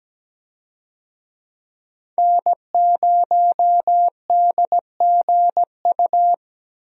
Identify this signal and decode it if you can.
Morse code, heard as a keyed tone, about 17 wpm, 700 Hz. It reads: N0DGU
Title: USA Call Signs - 5 Characters - Repeated 3x - 17wpm